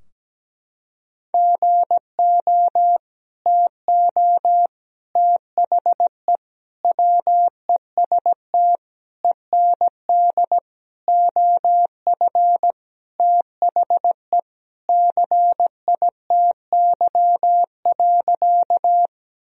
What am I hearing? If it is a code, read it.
Morse code, heard as a keyed tone, about 17 wpm, 700 Hz. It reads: GO TO THE WEST END OF THE CITY.
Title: Sentences from Top 1000 Words - Review Part 2 - 17wpm